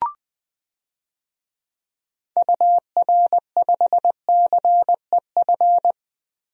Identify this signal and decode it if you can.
Morse code, heard as a keyed tone, about 20 wpm, 700 Hz. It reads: UR5CEF
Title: DXCC-100 Call Signs - 6 Characters - Repeated 3x - 20wpm